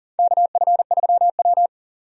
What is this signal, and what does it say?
Morse code, heard as a keyed tone, about 40 wpm, 700 Hz. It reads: KF3W
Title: USA Call Signs - 4 Characters - 40wpm